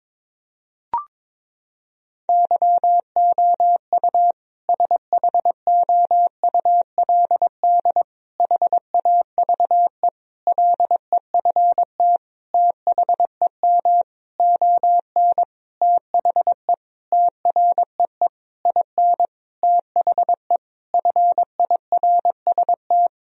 Morse code, heard as a keyed tone, about 22 wpm, 700 Hz. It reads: YOU SHOULD HAVE LEFT THEM ON THE TREE IN THE FIRST
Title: Sentences from Top 1000 Words - Review Part 1 - 22wpm